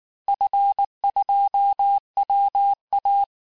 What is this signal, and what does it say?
Morse code, heard as a keyed tone, about 19 wpm, 800 Hz. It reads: F2WA
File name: cours-du-lundi files / cours-1595--FIN-de-QSO-vitesse-18mpm